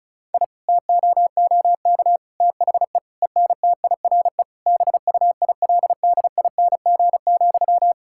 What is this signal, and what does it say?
Morse code, heard as a keyed tone, about 36 wpm, 700 Hz. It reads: I TOOK THE ENTIRE BUILDING,
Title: Sentences from Top 1000 Words - Review Part 5 - 35wpm